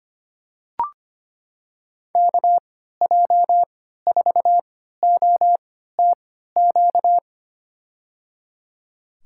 Morse code, heard as a keyed tone, about 25 wpm, spaced as for 14 wpm, 700 Hz. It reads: KJ4OTQ
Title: USA Call Signs - 6 Characters - 3x Character Spacing - 25wpm